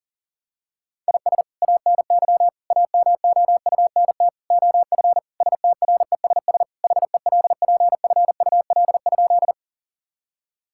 Morse code, heard as a keyed tone, about 37 wpm, 700 Hz. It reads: IS ANY AMOUNT OF STRESS HELPFUL?
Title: Sentences from Top 1000 Words - Review Part 6 - 40wpm